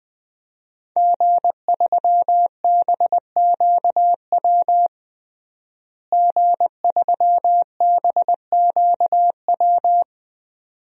Morse code, heard as a keyed tone, about 20 wpm, 700 Hz. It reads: G3BQW G3BQW
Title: DXCC-100 Call Signs - 5 Characters - Repeated 2x - 20wpm